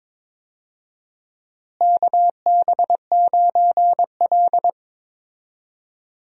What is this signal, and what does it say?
Morse code, heard as a keyed tone, about 22 wpm, 700 Hz. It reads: KB9L
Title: USA Call Signs - 4 Characters - Repeated 2x - 22wpm